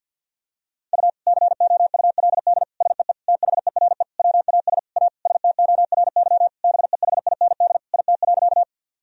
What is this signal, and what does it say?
Morse code, heard as a keyed tone, about 52 wpm, 700 Hz. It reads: U COULD SEE THERE WAS A STORY BEHIND IT.
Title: Sentences from Top 500 Words and Abbreviations 50wpm